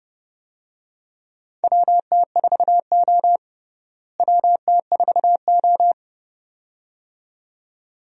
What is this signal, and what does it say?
Morse code, heard as a keyed tone, about 31 wpm, 700 Hz. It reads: WT4O WT4O
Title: USA Call Signs - 4 Characters - Repeated 2x - 30wpm